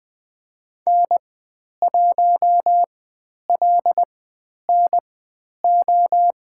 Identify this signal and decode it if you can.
Morse code, heard as a keyed tone, about 20 wpm, spaced as for 10 wpm, 700 Hz. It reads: N1LNO
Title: 095 Lesson 16g - Letter c - Call Signs 20wpm